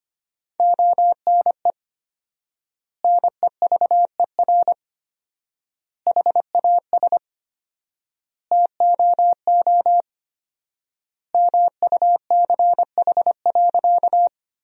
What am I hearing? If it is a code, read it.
Morse code, heard as a keyed tone, about 25 wpm, 700 Hz. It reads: ONE NEVER HAS TOO MUCH.
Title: 4x Word-Spacing - Sentences from Top 300 Words - 25wpm